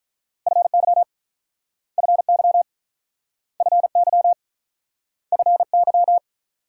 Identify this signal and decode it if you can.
Morse code, heard as a keyed tone, about 47 wpm, 700 Hz. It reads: FY FY FY FY